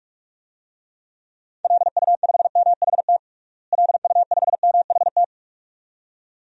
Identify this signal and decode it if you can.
Morse code, heard as a keyed tone, about 46 wpm, 700 Hz. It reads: LU5MHT LU5MHT
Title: DXCC-100 Call Signs - 6 Characters - Repeated 2x - 45wpm